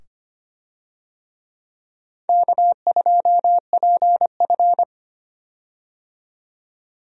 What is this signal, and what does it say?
Morse code, heard as a keyed tone, about 25 wpm, 700 Hz. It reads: K2PF
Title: USA Call Signs - 4 Characters - 25wpm